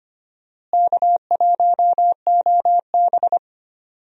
Morse code, heard as a keyed tone, about 25 wpm, 700 Hz. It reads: K1OB
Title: USA Call Signs - 4 Characters - Repeated 2x - 25wpm